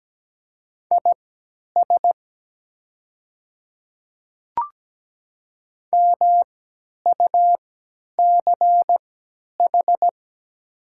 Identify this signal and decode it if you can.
Morse code, heard as a keyed tone, about 17 wpm, spaced as for 10 wpm, 700 Hz. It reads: IS MUCH
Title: 3x Character Spacing - Top 300 Words - 17wpm